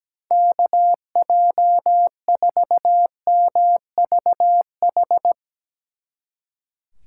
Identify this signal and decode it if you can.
Morse code, heard as a keyed tone, about 18 wpm, 700 Hz. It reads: KJ4MVH
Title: Speed Racing - USA Call Signs 17wpm